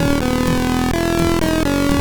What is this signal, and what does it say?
There is no code.